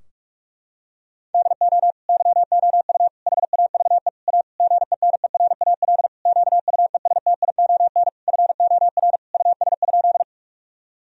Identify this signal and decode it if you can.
Morse code, heard as a keyed tone, about 46 wpm, 700 Hz. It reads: DO YOU HAVE A GENERAL QUESTION FOR US?
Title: Sentences from Top 1000 Words - Review Part 3 - 45wpm